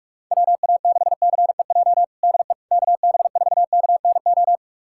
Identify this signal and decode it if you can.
Morse code, heard as a keyed tone, about 46 wpm, 700 Hz. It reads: WA6CEJ DE KB4KNO
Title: USA Callsign DE Callsign - 6 Characters - 45wpm